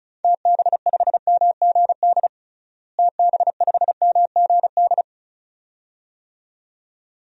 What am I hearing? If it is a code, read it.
Morse code, heard as a keyed tone, about 36 wpm, 700 Hz. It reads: TB5MGD TB5MGD
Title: DXCC-100 Call Signs - 6 Characters - Repeated 2x - 35wpm